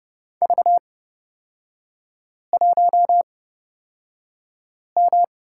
Morse code, heard as a keyed tone, about 31 wpm, spaced as for 5 wpm, 690 Hz. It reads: V1M
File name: Lesson 37e - Letter x - Three Characters 30wpm